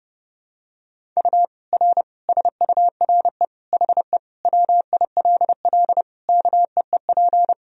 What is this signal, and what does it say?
Morse code, heard as a keyed tone, about 30 wpm, 700 Hz. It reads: U R SURE HE WILL KEEP
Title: Sentences from Top 300 Words and Abbreviations 30wpm